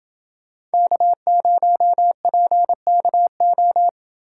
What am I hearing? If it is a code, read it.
Morse code, heard as a keyed tone, about 28 wpm, 700 Hz. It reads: K0PKO